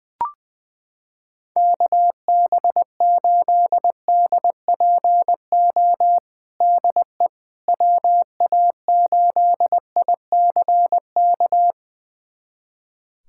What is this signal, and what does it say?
Morse code, heard as a keyed tone, about 20 wpm, 700 Hz. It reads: KB8DPO DE WA8ICK
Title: USA Callsign DE Callsign - 6 Characters - 20wpm